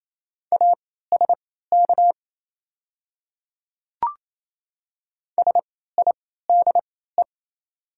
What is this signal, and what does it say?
Morse code, heard as a keyed tone, about 28 wpm, spaced as for 16 wpm, 700 Hz. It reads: ASK SIDE
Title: 3x Character Spacing - Top 300 Words - 28wpm